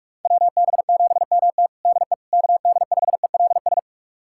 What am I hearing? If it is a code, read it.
Morse code, heard as a keyed tone, about 46 wpm, 700 Hz. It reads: WB7MT DE KD5ELS
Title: USA Callsign DE Callsign 45wpm